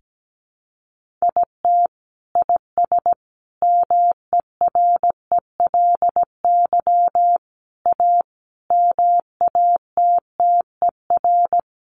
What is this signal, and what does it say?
Morse code, heard as a keyed tone, about 17 wpm, 700 Hz. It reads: IT IS MERELY A MATTER
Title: Sentences from Top 1400 Words 17wpm